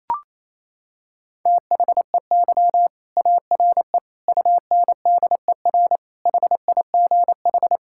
Morse code, heard as a keyed tone, about 28 wpm, 700 Hz. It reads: THEY ARE UNDER HIGH
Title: Sentences from Top 1600 Words 28wpm